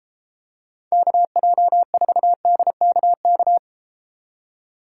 Morse code, heard as a keyed tone, about 32 wpm, 700 Hz. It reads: KJ4DKK